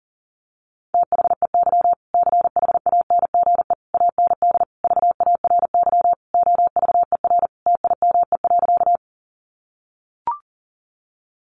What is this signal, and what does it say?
Morse code, heard as a keyed tone, about 40 wpm, 700 Hz. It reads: THEY CHANGE AND VARY OVER TIME.
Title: Sentences from Top 1200 Words 40wpm